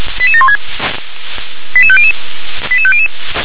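Five-tone selective call (ZVEI), heard as quick runs of five tones, tones of about 70 ms. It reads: 88716 88500 88500
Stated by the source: Five-tone ZVEI selective call received on an SDR.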